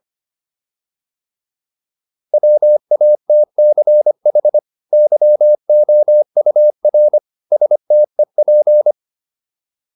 Morse code, heard as a keyed tone, about 25 wpm, 600 Hz. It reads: WATCH YOUR STEP